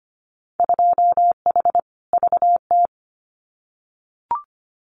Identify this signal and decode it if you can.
Morse code, heard as a keyed tone, about 25 wpm, 700 Hz. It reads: 2H VT
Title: Winter Field Day Exchanges 25wpm